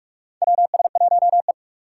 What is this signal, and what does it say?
Morse code, heard as a keyed tone, about 44 wpm, 700 Hz. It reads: WS1E
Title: USA Call Signs - 4 Characters - 4x Character Spacing - 45wpm